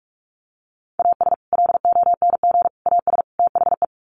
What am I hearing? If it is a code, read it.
Morse code, heard as a keyed tone, about 44 wpm, 700 Hz. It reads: AS LONG AS THE
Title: Sets of 4 Words 45wpm